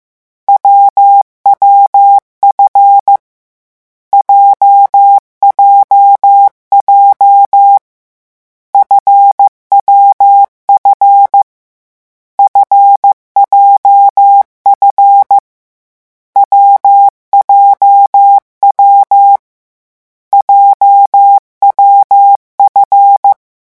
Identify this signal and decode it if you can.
Morse code, heard as a keyed tone, about 15 wpm, 800 Hz. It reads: WWF JJJ FWF FJF WJW JWF
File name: morse mp3 / lecon7-vitesse 14